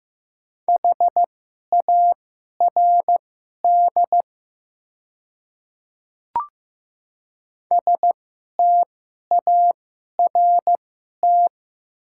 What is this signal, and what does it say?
Morse code, heard as a keyed tone, about 15 wpm, spaced as for 11 wpm, 700 Hz. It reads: HARD START
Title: 2x Character Spacing - Top 500 Words - 15wpm